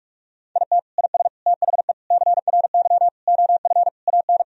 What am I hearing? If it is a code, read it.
Morse code, heard as a keyed tone, about 46 wpm, 700 Hz. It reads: IT IS THE CRY OF AN
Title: Sentences from Top 1000 Words - Review Part 2 - 45wpm